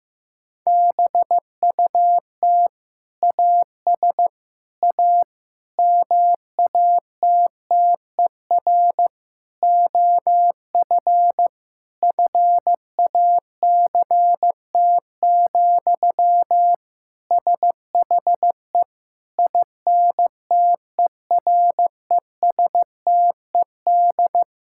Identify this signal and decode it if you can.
Morse code, heard as a keyed tone, about 15 wpm, 700 Hz. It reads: BUT AS A MATTER OF FACT, SHE INTERESTED
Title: Sentences from Top 1000 Words - Review Part 6 - 15wpm